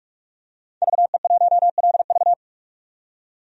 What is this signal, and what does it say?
Morse code, heard as a keyed tone, about 44 wpm, 700 Hz. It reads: VE1LV
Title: DXCC-100 Call Signs - 5 Characters - 45wpm